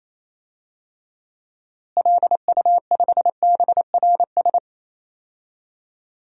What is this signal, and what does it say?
Morse code, heard as a keyed tone, about 27 wpm, 700 Hz. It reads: LU5BRS
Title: DXCC-100 Call Signs - 6 Characters - Repeated 3x - 28wpm